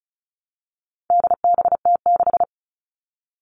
Morse code, heard as a keyed tone, about 34 wpm, 700 Hz. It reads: DBT6
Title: All Cut Numbers - 4-Digit 35wpm